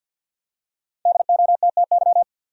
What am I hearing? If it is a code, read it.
Morse code, heard as a keyed tone, about 53 wpm, 700 Hz. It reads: DOTTY
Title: USA Common Names - Max 5 Letters - 50wpm